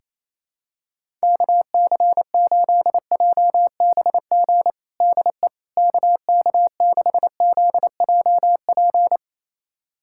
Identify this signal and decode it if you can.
Morse code, heard as a keyed tone, about 28 wpm, 700 Hz. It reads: KC8JBG DE KK6ZJP
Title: USA Callsign DE Callsign - 6 Characters - 28wpm